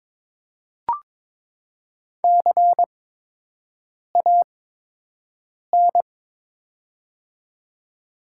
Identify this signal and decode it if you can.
Morse code, heard as a keyed tone, about 21 wpm, spaced as for 6 wpm, 700 Hz. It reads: CAN
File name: Top 90 QSO Elements - 8x Character Spacing - 22wpm